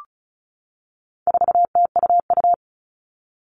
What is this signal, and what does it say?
Morse code, heard as a keyed tone, about 34 wpm, 700 Hz. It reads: 4TUU
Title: All Cut Numbers - 4-Digit 35wpm